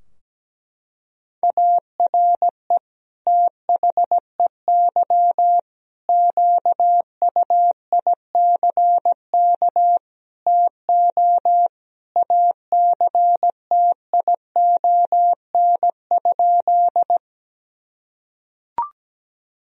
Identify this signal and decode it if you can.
Morse code, heard as a keyed tone, about 17 wpm, 700 Hz. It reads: ARE THEY QUICK TO ACTION?
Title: Sentences from Top 1000 Words - Review Part 4 - 17wpm